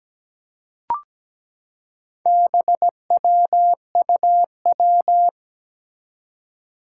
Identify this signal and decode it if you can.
Morse code, heard as a keyed tone, about 17 wpm, 700 Hz. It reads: BWUW
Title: Four Challenging Letter-Number Combinations 17wpm